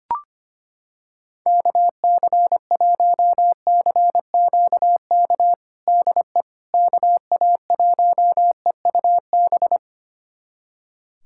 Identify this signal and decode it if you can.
Morse code, heard as a keyed tone, about 25 wpm, 700 Hz. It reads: KC1CQK DE KA1EUB
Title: USA Callsign DE Callsign 25wpm